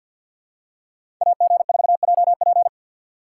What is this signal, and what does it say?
Morse code, heard as a keyed tone, about 53 wpm, 700 Hz. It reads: AG4JP
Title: USA Call Signs - 3x Character Spacing - 50wpm